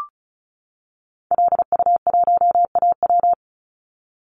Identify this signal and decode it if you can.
Morse code, heard as a keyed tone, about 36 wpm, 700 Hz. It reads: LU1AW
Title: Call Signs - 35wpm